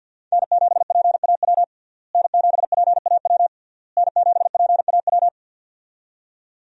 Morse code, heard as a keyed tone, about 49 wpm, 700 Hz. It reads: N7PAW N7PAW N7PAW
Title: USA Call Signs - 5 Characters - Repeated 3x - 50wpm